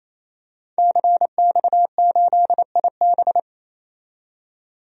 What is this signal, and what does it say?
Morse code, heard as a keyed tone, about 27 wpm, 700 Hz. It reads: CX8IB
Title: DXCC-100 Call Signs - 5 Characters - 28wpm